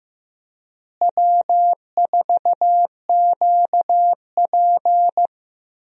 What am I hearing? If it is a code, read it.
Morse code, heard as a keyed tone, about 15 wpm, 700 Hz. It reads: W4QP
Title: USA Call Signs - 4 Characters - Repeated 3x - 15wpm